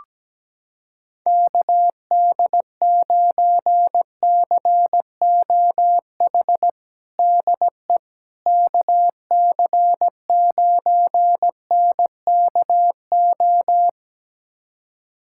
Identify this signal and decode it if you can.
Morse code, heard as a keyed tone, about 17 wpm, 700 Hz. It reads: KD9COH DE KC9NKO
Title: USA Callsign DE Callsign - 6 Characters - 17wpm